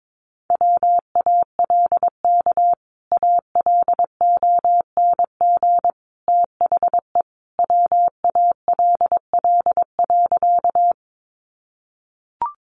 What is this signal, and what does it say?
Morse code, heard as a keyed tone, about 22 wpm, 700 Hz. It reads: WALK ALONG THE WALL.
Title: Sentences from Top 1100 Words 22wpm